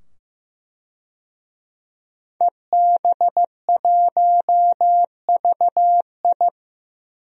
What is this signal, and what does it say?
Morse code, heard as a keyed tone, about 15 wpm, 700 Hz. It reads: EB1VI
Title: DXCC-100 Call Signs - 5 Characters - 15wpm